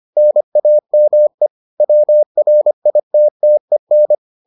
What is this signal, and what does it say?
Morse code, heard as a keyed tone, about 25 wpm, 600 Hz. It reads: NAME WRITTEN